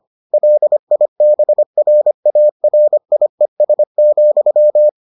Morse code, heard as a keyed tone, about 25 wpm, 600 Hz. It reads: LIBRARIES,